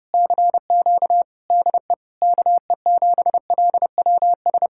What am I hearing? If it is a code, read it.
Morse code, heard as a keyed tone, about 33 wpm, 700 Hz. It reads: CQ DE KE7LWS